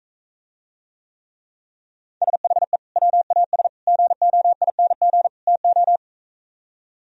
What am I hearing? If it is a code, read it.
Morse code, heard as a keyed tone, about 45 wpm, 700 Hz. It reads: SHE WAS GOING TO